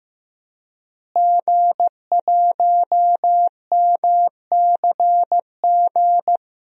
Morse code, heard as a keyed tone, about 15 wpm, 700 Hz. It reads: G1MCG
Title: DXCC-100 Call Signs - 5 Characters - Repeated 3x - 15wpm